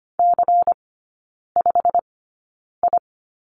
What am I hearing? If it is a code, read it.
Morse code, heard as a keyed tone, about 25 wpm, spaced as for 9 wpm, 700 Hz. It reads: C5I
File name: course / 25:10wpm / 093 Lesson 16e - Letter c - Three Characters 25wpm